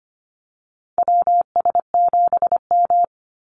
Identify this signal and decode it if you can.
Morse code, heard as a keyed tone, about 25 wpm, 700 Hz. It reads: WS7M
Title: Call Signs - 25wpm